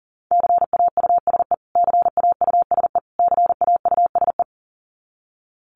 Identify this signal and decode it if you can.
Morse code, heard as a keyed tone, about 37 wpm, 700 Hz. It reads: CAUSE CAUSE CAUSE
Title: Top 500 Words Repeated 40wpm